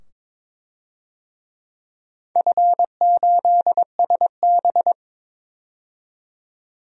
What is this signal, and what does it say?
Morse code, heard as a keyed tone, about 22 wpm, 700 Hz. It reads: F8SB
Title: DXCC-100 Call Signs - 4 Characters - 22wpm